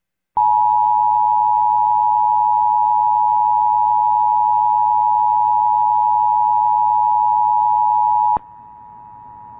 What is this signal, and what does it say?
Emergency Alert System (SAME) tones (the 853 and 960 Hz attention tone). It is heard from 0.3 to 8.4 s.